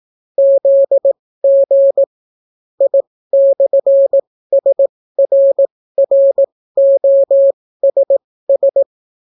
Morse code, heard as a keyed tone, about 18 wpm, spaced as for 15 wpm, 550 Hz. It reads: ZG I/SRROSS